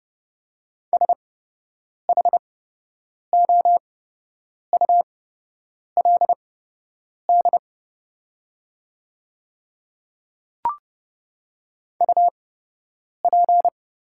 Morse code, heard as a keyed tone, about 30 wpm, spaced as for 8 wpm, 700 Hz. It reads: SHOULD UP
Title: 8x Character Spacing - Top 300 Words - 30wpm